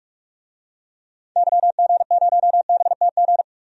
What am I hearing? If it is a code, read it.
Morse code, heard as a keyed tone, about 43 wpm, 700 Hz. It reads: YG0BTG
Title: DXCC-100 Call Signs - 6 Characters - 45wpm